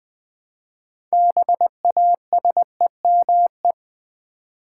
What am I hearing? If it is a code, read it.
Morse code, heard as a keyed tone, about 21 wpm, 690 Hz. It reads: BASEME
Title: Build A Word - 3 Letters At A Time - 8-Letter Words - 20wpm